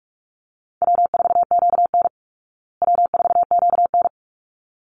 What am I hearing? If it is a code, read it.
Morse code, heard as a keyed tone, about 44 wpm, 700 Hz. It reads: W4QN W4QN
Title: Easy Call Signs Repeated 2x - 45wpm